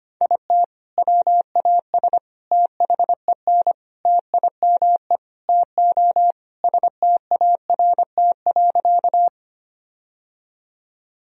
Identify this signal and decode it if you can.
Morse code, heard as a keyed tone, about 25 wpm, 700 Hz. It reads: IT WAS THEN TIME TO START.
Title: Sentences from Top 1000 Words - Review Part 1 - 25wpm